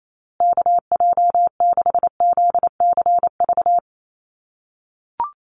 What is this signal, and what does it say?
Morse code, heard as a keyed tone, about 28 wpm, 700 Hz. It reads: KJ6ZCV
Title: USA Call Signs Repeated 3x 28wpm